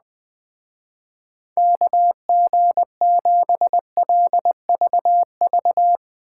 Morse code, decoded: KG7LVV

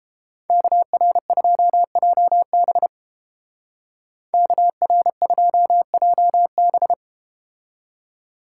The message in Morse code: KR2JB KR2JB

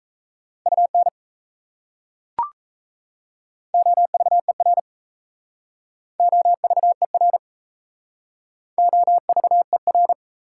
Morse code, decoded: UN OVER OVER OVER

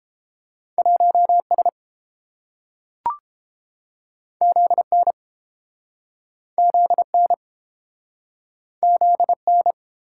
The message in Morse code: 1S ZN ZN ZN

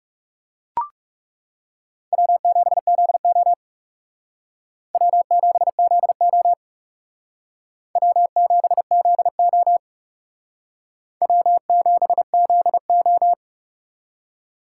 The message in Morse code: W7ZO W7ZO W7ZO W7ZO

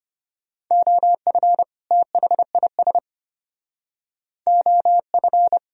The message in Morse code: OF THIS OF

